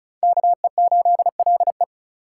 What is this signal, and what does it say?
Morse code, heard as a keyed tone, about 45 wpm, 700 Hz. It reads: KE8LE